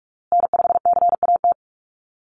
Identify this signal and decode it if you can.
Morse code, heard as a keyed tone, about 44 wpm, 690 Hz. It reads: N5CAT
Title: USA Call Signs Repeated 2x 45wpm